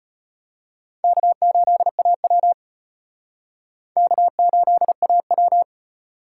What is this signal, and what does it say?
Morse code, heard as a keyed tone, about 38 wpm, 700 Hz. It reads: K8AW K8AW